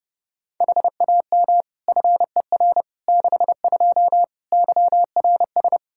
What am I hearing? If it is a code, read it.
Morse code, heard as a keyed tone, about 32 wpm, 700 Hz. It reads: HAM FER 62 YRS
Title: K1USN SST QSO Practice 30wpm